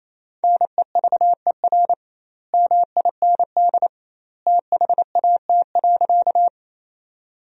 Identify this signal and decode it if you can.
Morse code, heard as a keyed tone, about 28 wpm, 700 Hz. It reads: NEVER MIND THAT.
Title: 2x Word-Spacing - Sentences from Top 500 Words - 28wpm